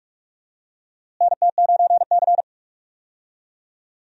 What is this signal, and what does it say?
Morse code, heard as a keyed tone, about 44 wpm, 700 Hz. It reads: NT9C
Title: USA Call Signs - 4 Characters - Repeated 2x - 45wpm